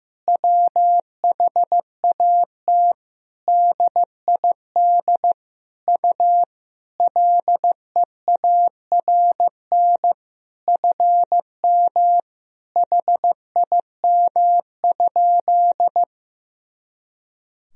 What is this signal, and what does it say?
Morse code, heard as a keyed tone, about 15 wpm, 700 Hz. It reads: WHAT DID U LEARN FM HIM?
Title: Sentences from Top 300 Words and Abbreviations 15wpm